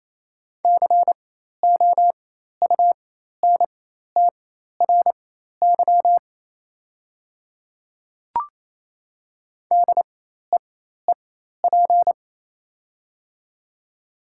Morse code, heard as a keyed tone, about 27 wpm, spaced as for 13 wpm, 690 Hz. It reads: COUNTRY DEEP
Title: 4x Character Spacing - Top 500 Words - 28wpm